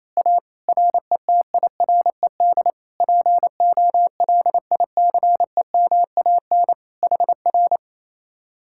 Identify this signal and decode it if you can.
Morse code, heard as a keyed tone, about 27 wpm, 700 Hz. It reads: A RETIRED POLICEMAN HR
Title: Speed-Racing Common QSO Phrases Encore - 28wpm